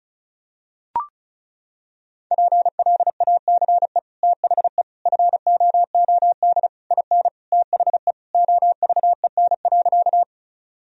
Morse code, heard as a keyed tone, about 36 wpm, 700 Hz. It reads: PLACE THE FOOD IN THE OVEN.
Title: Sentences from Top 1000 Words - Review Part 4 - 35wpm